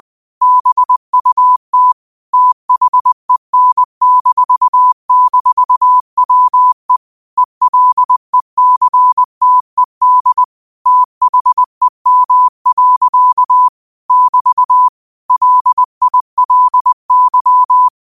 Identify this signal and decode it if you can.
Morse code, heard as a keyed tone, about 20 wpm, 1000 Hz. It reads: BUT THEN--WE ELECTED THEM. <BT> LILY